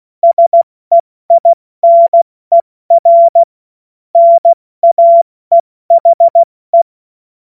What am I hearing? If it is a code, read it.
Morse code, heard as a keyed tone, about 16 wpm, 680 Hz. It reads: SEINER NAEHE
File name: Morsekurs / Witze / Witz 04. Januar 2022, 70 F80